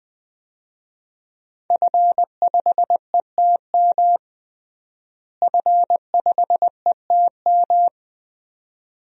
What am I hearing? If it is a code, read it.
Morse code, heard as a keyed tone, about 20 wpm, 700 Hz. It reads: F5ETM F5ETM
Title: DXCC-100 Call Signs - 5 Characters - Repeated 2x - 20wpm